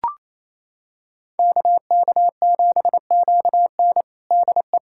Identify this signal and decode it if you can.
Morse code, heard as a keyed tone, about 28 wpm, 700 Hz. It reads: KK7QN DE
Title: USA Callsign DE Callsign - 5 Characters - 28wpm